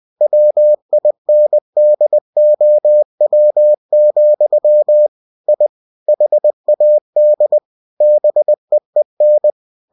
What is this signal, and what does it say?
Morse code, heard as a keyed tone, about 20 wpm, 600 Hz. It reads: WINDOW, I HAD BEEN
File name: cw-books / winnie-the-pooh / winnie-20wpm-page-0090